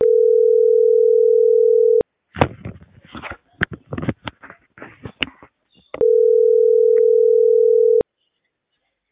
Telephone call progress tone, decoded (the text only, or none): ringback tone